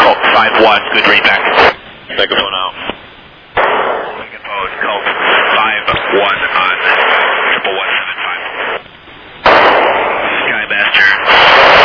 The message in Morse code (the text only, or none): none